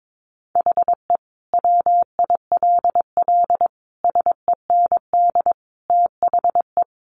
Morse code, heard as a keyed tone, about 22 wpm, 700 Hz. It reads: HE WILL SEND THE
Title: Additional Easy Phrases 22wpm